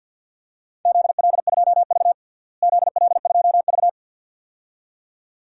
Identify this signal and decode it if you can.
Morse code, heard as a keyed tone, about 53 wpm, 700 Hz. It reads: ZL2V ZL2V
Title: DXCC-100 Call Signs - 4 Characters - Repeated 2x - 50wpm